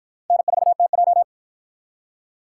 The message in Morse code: N4TJ

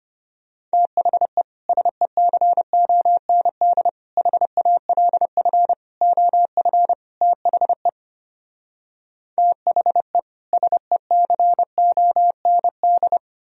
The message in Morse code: THE SECOND HALF OF THE THE SECOND

